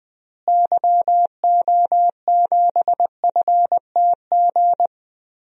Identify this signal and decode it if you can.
Morse code, heard as a keyed tone, about 20 wpm, 700 Hz. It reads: YO7FTG
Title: DXCC-100 Call Signs - 6 Characters - Repeated 3x - 20wpm